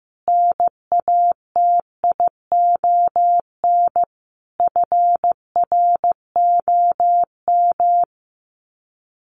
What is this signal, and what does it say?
Morse code, heard as a keyed tone, about 15 wpm, 700 Hz. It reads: NATION FROM
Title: Sets of 5 Words 15wpm